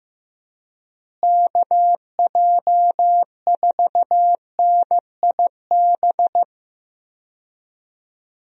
Morse code, heard as a keyed tone, about 15 wpm, 700 Hz. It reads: KJ4NIB
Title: USA Call Signs - 6 Characters - Repeated 3x - 15wpm